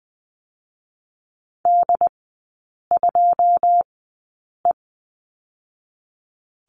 Morse code, heard as a keyed tone, about 20 wpm, spaced as for 8 wpm, 690 Hz. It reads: D2E
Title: 081 Lesson 14e - Number 2 - Three Characters 20wpm